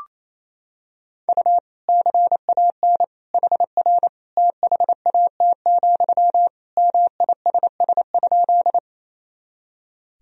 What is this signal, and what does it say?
Morse code, heard as a keyed tone, about 28 wpm, 700 Hz. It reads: U CAN HR THAT, MISS?
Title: Sentences from Top 500 Words and Abbreviations 28wpm